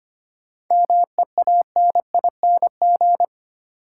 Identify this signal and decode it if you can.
Morse code, heard as a keyed tone, about 25 wpm, 700 Hz. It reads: MEANING
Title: Sentences from Top 900 Words - New Word Drill - 25wpm